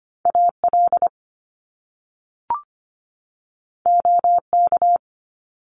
Morse code, heard as a keyed tone, about 25 wpm, 700 Hz. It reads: AL OK
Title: Top Two Letter Combinations in Words 25wpm